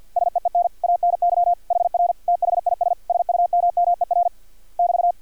Morse code, heard as a keyed tone, about 50 wpm, 700 Hz. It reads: SEEN ANY HR THIS SUMMER <BT>
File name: archived / CRSnet